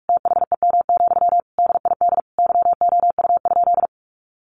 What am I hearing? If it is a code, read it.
Morse code, heard as a keyed tone, about 44 wpm, 700 Hz. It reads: THEM, DID YOU?
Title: Sentences from Top 1500 Words 45wpm